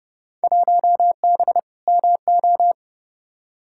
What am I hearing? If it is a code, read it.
Morse code, heard as a keyed tone, about 31 wpm, 700 Hz. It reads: 1B MO
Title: Field Day Exchanges - 30wpm